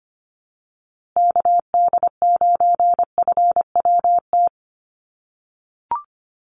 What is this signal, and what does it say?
Morse code, heard as a keyed tone, about 25 wpm, 700 Hz. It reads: KD9FWT
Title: USA Call Signs Repeated 2x 25wpm